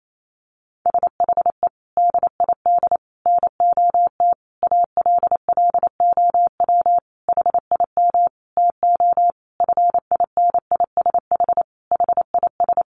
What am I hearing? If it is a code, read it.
Morse code, heard as a keyed tone, about 28 wpm, 700 Hz. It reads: SHE DID NOT ALLOW HIM TO FINISH HIS
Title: Sentences from Top 1200 Words 28wpm